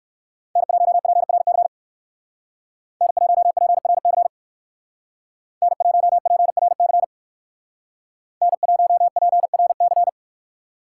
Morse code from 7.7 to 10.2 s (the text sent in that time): N1PRC